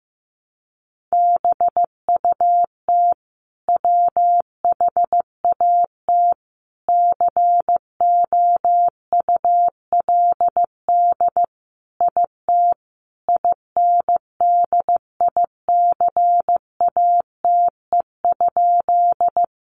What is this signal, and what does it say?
Morse code, heard as a keyed tone, about 15 wpm, 700 Hz. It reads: BUT WHAT COULD IT INDICATE?
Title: Sentences from Top 1300 Words 15wpm